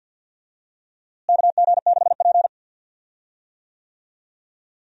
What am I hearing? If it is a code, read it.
Morse code, heard as a keyed tone, about 51 wpm, 700 Hz. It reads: KG6P
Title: USA Call Signs - 4 Characters - Repeated 2x - 50wpm